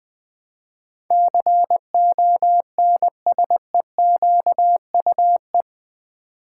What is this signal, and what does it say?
Morse code, heard as a keyed tone, about 20 wpm, 700 Hz. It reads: CONSEQUE
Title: Build A Word - 4 Letters At A Time - 11-Letter Words - 20wpm